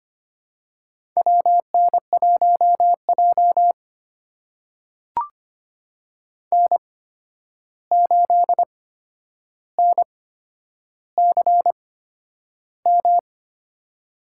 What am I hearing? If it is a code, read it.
Morse code, heard as a keyed tone, about 25 wpm, spaced as for 10 wpm, 700 Hz. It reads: WN1J N8NCM
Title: USA Call Signs - 8x Character Spacing - 25wpm